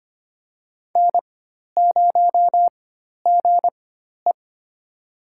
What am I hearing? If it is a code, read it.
Morse code, heard as a keyed tone, about 25 wpm, spaced as for 12 wpm, 700 Hz. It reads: N0GE
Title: USA Call Signs - 4 Characters - 4x Character Spacing - 25wpm